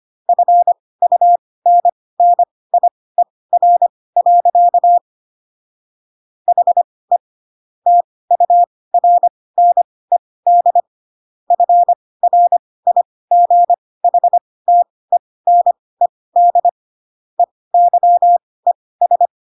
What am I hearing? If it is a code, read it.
Morse code, heard as a keyed tone, about 25 wpm, spaced as for 18 wpm, 700 Hz. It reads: FUNNIER. HE TURNED FRIGHTENED EYES
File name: Pinocchio / Pinocchio ch1 12 25 ik1hge